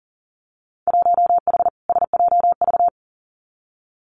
Morse code, heard as a keyed tone, about 40 wpm, 700 Hz. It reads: 1H SJV
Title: Winter Field Day Exchanges 40wpm